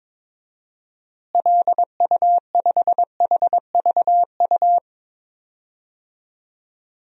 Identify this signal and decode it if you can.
Morse code, heard as a keyed tone, about 22 wpm, 700 Hz. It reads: LU5HVU
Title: DXCC-100 Call Signs - 6 Characters - Repeated 2x - 22wpm